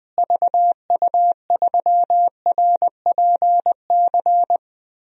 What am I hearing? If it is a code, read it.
Morse code, heard as a keyed tone, about 20 wpm, 700 Hz. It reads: VU3RPC
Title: DXCC-100 Call Signs - 6 Characters - Repeated 2x - 20wpm